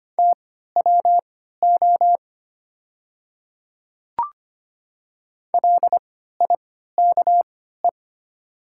Morse code, heard as a keyed tone, about 25 wpm, spaced as for 14 wpm, 700 Hz. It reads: TWO LIKE
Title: 3x Character Spacing - Top 100 Words - 25wpm